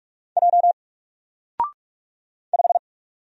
Morse code, heard as a keyed tone, about 44 wpm, 700 Hz. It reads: J 5
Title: Single Letter-Number 45wpm